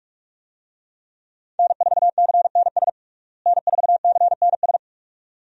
Morse code, heard as a keyed tone, about 42 wpm, 700 Hz. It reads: N4CNS N4CNS
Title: USA Call Signs - 5 Characters - Repeated 2x - 45wpm